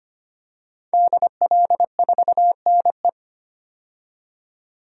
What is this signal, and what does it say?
Morse code, heard as a keyed tone, about 25 wpm, 700 Hz. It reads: DL4NE
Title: DXCC-100 Call Signs - 5 Characters - 25wpm